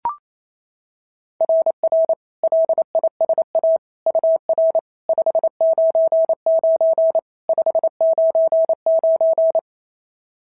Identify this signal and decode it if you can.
Morse code, heard as a keyed tone, about 28 wpm, 690 Hz. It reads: RR LISA UR 599 599
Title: SOTA QSO Practice 28wpm